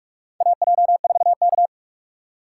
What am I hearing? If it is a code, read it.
Morse code, heard as a keyed tone, about 44 wpm, 700 Hz. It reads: AJ4K
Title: USA Call Signs - 4 Characters - 4x Character Spacing - 45wpm